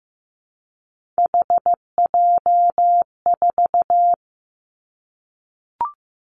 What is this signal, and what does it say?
Morse code, heard as a keyed tone, about 15 wpm, 700 Hz. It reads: HJ4
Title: Callsign Prefixes for Top 100 DXCC - 15wpm